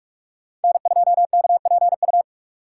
Morse code, heard as a keyed tone, about 46 wpm, 700 Hz. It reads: N2KPU
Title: USA Call Signs - 5 Characters - 4x Character Spacing - 45wpm